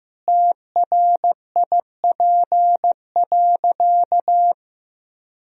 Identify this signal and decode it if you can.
Morse code, heard as a keyed tone, about 16 wpm, 700 Hz. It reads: TRIP.
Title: Sentences from Top 1000 Words - Review Part 4 - 15wpm